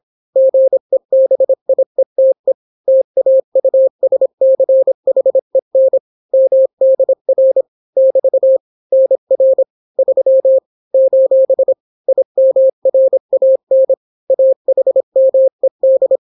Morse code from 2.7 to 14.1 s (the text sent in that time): TAUSCHEN MDR <BT> NR 3 : IMRAN